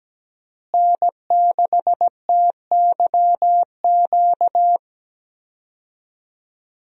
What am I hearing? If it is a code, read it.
Morse code, heard as a keyed tone, about 17 wpm, 700 Hz. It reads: N6TYQ
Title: USA Call Signs - 5 Characters - Repeated 2x - 17wpm